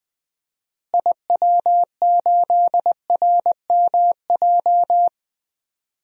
Morse code, heard as a keyed tone, about 20 wpm, 700 Hz. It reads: IW8RMJ
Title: DXCC-100 Call Signs - 6 Characters - Repeated 2x - 20wpm